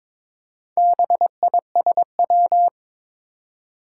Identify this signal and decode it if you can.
Morse code, heard as a keyed tone, about 22 wpm, 700 Hz. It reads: BISW
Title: Four Challenging Letter-Number Combinations 22wpm